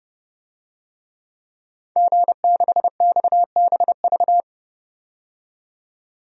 Morse code, heard as a keyed tone, about 30 wpm, 700 Hz. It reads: G6XBV
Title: DXCC-100 Call Signs - 5 Characters - 30wpm